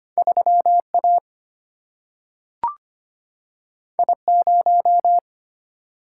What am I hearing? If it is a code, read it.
Morse code, heard as a keyed tone, about 25 wpm, 700 Hz. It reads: 3 A I 0
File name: Two Letter-Number Combinations - 25wpm